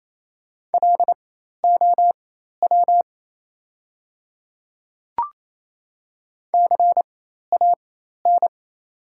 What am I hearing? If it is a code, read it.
Morse code, heard as a keyed tone, about 28 wpm, spaced as for 13 wpm, 700 Hz. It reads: LOW CAN
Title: 4x Character Spacing - Top 300 Words - 28wpm